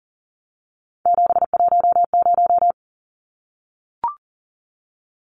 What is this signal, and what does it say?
Morse code, heard as a keyed tone, about 40 wpm, 700 Hz. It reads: 710
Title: Numbers - 3-Digit 40wpm